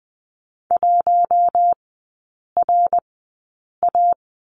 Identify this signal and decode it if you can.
Morse code, heard as a keyed tone, about 20 wpm, spaced as for 8 wpm, 700 Hz. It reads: 1RA
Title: 093 Lesson 16e - Letter c - Three Characters 20wpm